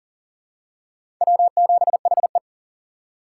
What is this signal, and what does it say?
Morse code, heard as a keyed tone, about 40 wpm, 700 Hz. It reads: W7HE
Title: USA Call Signs - 4 Characters - 3x Character Spacing - 40wpm